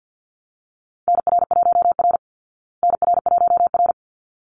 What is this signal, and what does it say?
Morse code, heard as a keyed tone, about 50 wpm, 700 Hz. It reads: NR1R NR1R
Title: Call Signs Repeated 3x - 50wpm